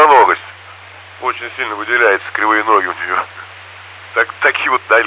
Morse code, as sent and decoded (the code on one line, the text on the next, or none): none